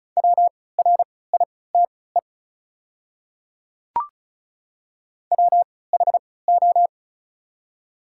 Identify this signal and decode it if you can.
Morse code, heard as a keyed tone, about 34 wpm, spaced as for 20 wpm, 690 Hz. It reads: WRITE WHO
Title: 3x Character Spacing - Top 100 Words - 35wpm